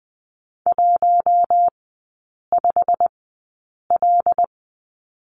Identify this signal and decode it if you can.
Morse code, heard as a keyed tone, about 20 wpm, spaced as for 8 wpm, 700 Hz. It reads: 15L
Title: 114 Lesson 19e - Letter w - Three Characters 20wpm